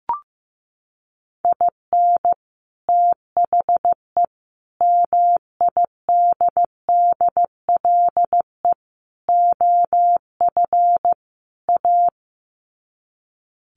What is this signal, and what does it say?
Morse code, heard as a keyed tone, about 15 wpm, 700 Hz. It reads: IN THE MIDDLE OF A
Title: Sets of 5 Words 15wpm